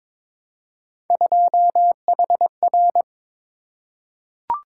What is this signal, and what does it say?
Morse code, heard as a keyed tone, about 22 wpm, 700 Hz. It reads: 2HR
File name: practice / 22wpm / Three Letter-Number Combinations - 22wpm